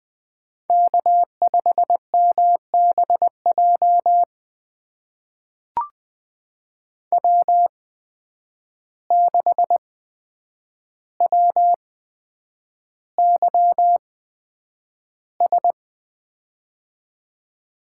Morse code, heard as a keyed tone, about 20 wpm, spaced as for 9 wpm, 700 Hz. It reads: K5MBJ W6WYS
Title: USA Call Signs - 5 Characters - 8x Character Spacing - 20wpm